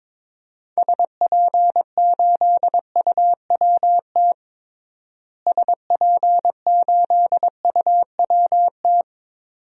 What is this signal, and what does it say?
Morse code, heard as a keyed tone, about 22 wpm, 700 Hz. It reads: SP8UWT SP8UWT
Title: DXCC-100 Call Signs - 6 Characters - Repeated 2x - 22wpm